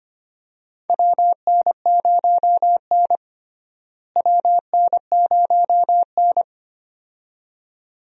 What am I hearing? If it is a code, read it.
Morse code, heard as a keyed tone, about 25 wpm, 700 Hz. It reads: WN0N WN0N
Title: USA Call Signs - 4 Characters - Repeated 2x - 25wpm